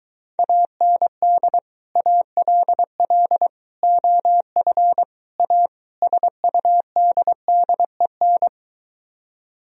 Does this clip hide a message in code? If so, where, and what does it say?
Yes: Morse code from 0.4 to 8.5 s, heard as a keyed tone, about 23 wpm, 700 Hz. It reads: AND ALL OF A SUDDEN